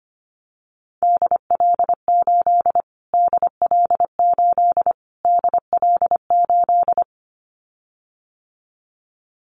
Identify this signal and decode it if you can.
Morse code, heard as a keyed tone, about 25 wpm, 700 Hz. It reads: DL8 DL8 DL8
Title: Callsign Prefixes for Top 25 DXCC - Repeated 3x - 25wpm